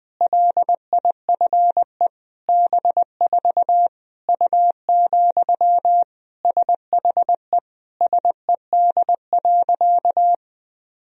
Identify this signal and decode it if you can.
Morse code, heard as a keyed tone, about 21 wpm, 700 Hz. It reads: LIFE B4 U, SHE SED.
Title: Sentences from Top 400 Words and Abbreviations 20wpm